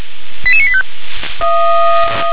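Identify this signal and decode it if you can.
Five-tone selective call (ZVEI), heard as quick runs of five tones, tones of about 70 ms. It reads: 88985
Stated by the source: ZVEI five-tone call received on an SDR.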